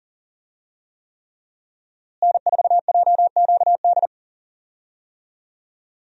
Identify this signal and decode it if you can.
Morse code, heard as a keyed tone, about 40 wpm, 700 Hz. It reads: N4JQD
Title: USA Call Signs - 5 Characters - Repeated 3x - 40wpm